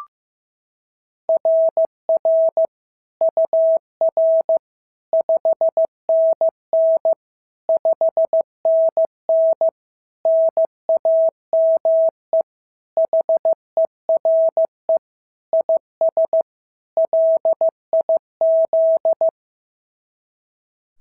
Morse code, heard as a keyed tone, about 15 wpm, 690 Hz. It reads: RR UR 5NN 5NN NAME HERE IS LIZ
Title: SOTA QSO Practice 15wpm